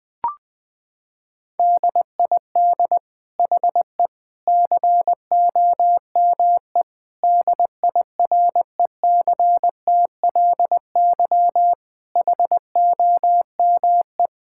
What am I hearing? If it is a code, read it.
Morse code, heard as a keyed tone, about 20 wpm, 700 Hz. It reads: DID HE COME DIRECTLY HOME
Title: Sentences from Top 1000 Words - Review Part 5 - 20wpm